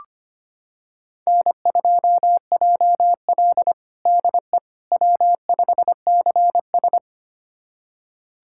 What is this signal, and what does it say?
Morse code, heard as a keyed tone, about 25 wpm, 700 Hz. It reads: N2JL DE W5CS
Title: USA Callsign DE Callsign - 4 Characters - 25wpm